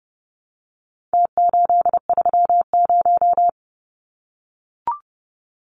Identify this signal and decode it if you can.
Morse code, heard as a keyed tone, about 30 wpm, 700 Hz. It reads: T830
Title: Numbers - Contest Style - 4-Digit 30wpm